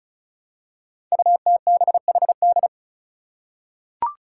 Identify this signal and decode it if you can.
Morse code, heard as a keyed tone, about 36 wpm, 700 Hz. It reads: UTBHD
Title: Five Challenging Letter-Number Combinations 35wpm